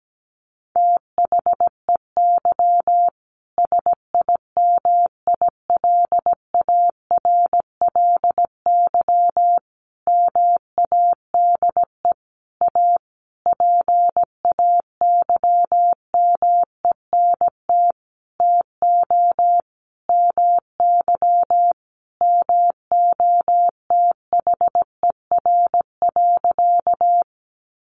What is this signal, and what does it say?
Morse code, heard as a keyed tone, about 17 wpm, 700 Hz. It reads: THEY SIMILARLY MADE A PAYMENT TO MY MOTHER.
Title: Sentences from Top 1400 Words 17wpm